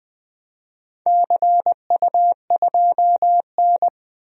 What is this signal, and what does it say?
Morse code, heard as a keyed tone, about 20 wpm, 700 Hz. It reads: CU2N
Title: DXCC-100 Call Signs - 4 Characters - Repeated 2x - 20wpm